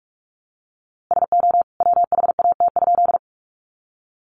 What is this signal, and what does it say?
Morse code, heard as a keyed tone, about 44 wpm, 700 Hz. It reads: SO WHAT?
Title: A Sound Of Thunder 45wpm